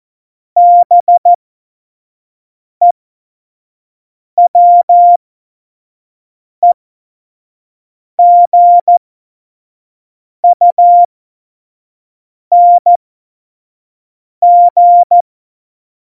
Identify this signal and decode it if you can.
Morse code, heard as a keyed tone, about 14 wpm, spaced as for 5 wpm, 700 Hz. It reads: BEWEGUNG
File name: Morsekurs / Witze / Witz 12. Januar 2021, 25 F70